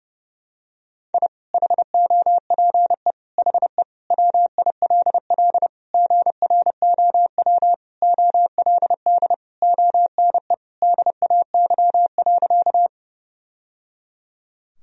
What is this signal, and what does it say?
Morse code, heard as a keyed tone, about 32 wpm, 700 Hz. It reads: I HOPE HE WILL GROW OLD ONE DAY.
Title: Sentences from Top 1000 Words - Review Part 1 - 30wpm